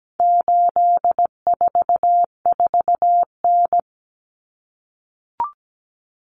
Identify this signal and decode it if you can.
Morse code, heard as a keyed tone, about 17 wpm, 700 Hz. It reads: 844N
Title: Common Cut Numbers - 4-Digit 17wpm